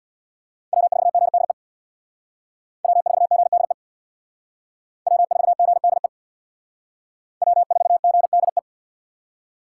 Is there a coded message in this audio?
Yes: Morse code from 0.7 to 8.6 s, heard as a keyed tone, about 64 wpm, 700 Hz. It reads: W4GDE W4GDE W4GDE W4GDE